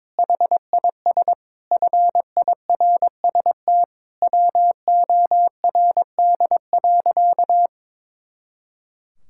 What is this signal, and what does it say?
Morse code, heard as a keyed tone, about 22 wpm, 700 Hz. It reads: HIS FIRST WORD.
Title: Pangram Challenge v2 - 22wpm